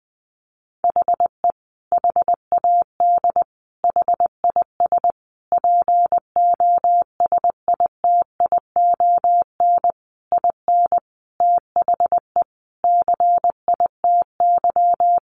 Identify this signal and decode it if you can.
Morse code, heard as a keyed tone, about 20 wpm, 700 Hz. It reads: HE HAD HIS POSITION IN THE CITY
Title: Sentences from Top 900 Words 20wpm